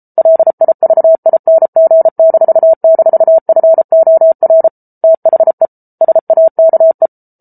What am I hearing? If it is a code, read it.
Morse code, heard as a keyed tone, about 34 wpm, 650 Hz. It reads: LIVING--FOR THE SAKE